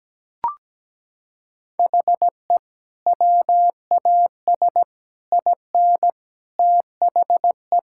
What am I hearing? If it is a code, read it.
Morse code, heard as a keyed tone, about 17 wpm, 700 Hz. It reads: HE WAS IN THE
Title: Sentences from Top 1600 Words 17wpm